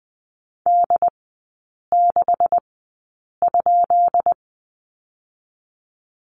Morse code, heard as a keyed tone, about 20 wpm, spaced as for 8 wpm, 690 Hz. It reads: D6?
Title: 159 Lesson 26e - Letter g - Three Characters 20wpm